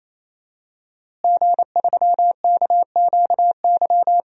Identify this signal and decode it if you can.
Morse code, heard as a keyed tone, about 28 wpm, 700 Hz. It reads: G3KQY
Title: DXCC-100 Call Signs - 5 Characters - Repeated 3x - 28wpm